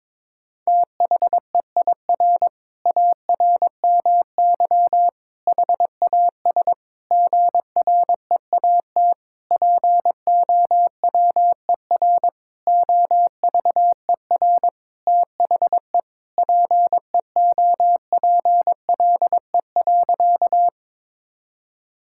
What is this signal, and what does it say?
Morse code, heard as a keyed tone, about 21 wpm, 700 Hz. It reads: THEIR ARMY HAS GREAT POWER OVER THE PEOPLE.
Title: Sentences from Top 1000 Words - Review Part 5 - 22wpm